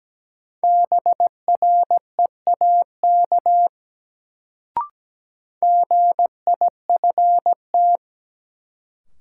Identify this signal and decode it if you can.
Morse code, heard as a keyed tone, about 17 wpm, 700 Hz. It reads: BREAK GIFT
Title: Sentences from Top 900 Words - New Word Drill - 17wpm